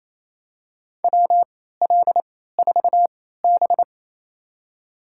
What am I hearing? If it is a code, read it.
Morse code, heard as a keyed tone, about 28 wpm, spaced as for 16 wpm, 700 Hz. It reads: WL4B
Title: USA Call Signs - 3x Character Spacing - 28wpm